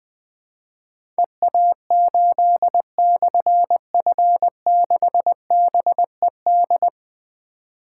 Morse code, heard as a keyed tone, about 20 wpm, 700 Hz. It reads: EA8/F6BED
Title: Difficult Call Signs 20wpm